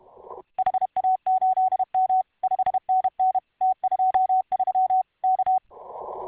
Morse code, heard as a keyed tone, about 31 wpm, 750 Hz. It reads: HA8M 5NN T23 K